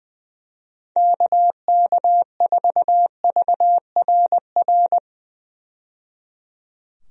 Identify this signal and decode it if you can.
Morse code, heard as a keyed tone, about 20 wpm, 700 Hz. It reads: KK4VRR
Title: USA Call Signs - 6 Characters - 20wpm